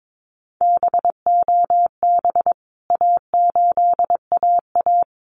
Morse code, heard as a keyed tone, about 22 wpm, 700 Hz. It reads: BOB A8AA
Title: CWT Exchange 22wpm